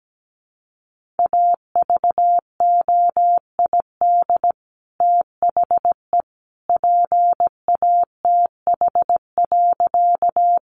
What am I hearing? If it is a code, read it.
Morse code, heard as a keyed tone, about 17 wpm, 700 Hz. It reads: AVOID THE PATH.
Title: Sentences from Top 1400 Words 17wpm